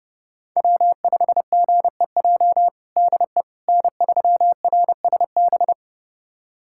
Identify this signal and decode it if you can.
Morse code, heard as a keyed tone, about 32 wpm, 700 Hz. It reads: W5GEJ DE N3RSB
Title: USA Callsign DE Callsign - 5 Characters - 30wpm